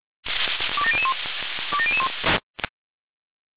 Five-tone selective call (ZVEI), heard as quick runs of five tones, tones of about 70 ms. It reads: 27001 27001